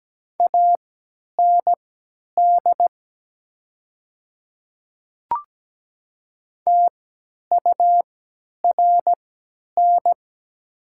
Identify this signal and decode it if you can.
Morse code, heard as a keyed tone, about 17 wpm, spaced as for 10 wpm, 700 Hz. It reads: AND TURN
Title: 3x Character Spacing - Top 300 Words - 17wpm